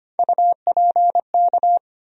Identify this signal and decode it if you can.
Morse code, heard as a keyed tone, about 25 wpm, 700 Hz. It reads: UPK
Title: USA Call Sign Postfixes - 3 Letters 25wpm